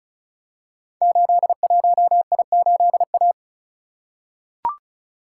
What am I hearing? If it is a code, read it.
Morse code, heard as a keyed tone, about 36 wpm, 690 Hz. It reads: 81I8A
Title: Code Groups - 5 Letter:Numbers - 35wpm